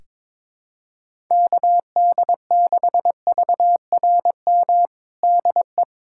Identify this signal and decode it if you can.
Morse code, heard as a keyed tone, about 22 wpm, 700 Hz. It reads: KD6VRM DE
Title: USA Callsign DE Callsign 22wpm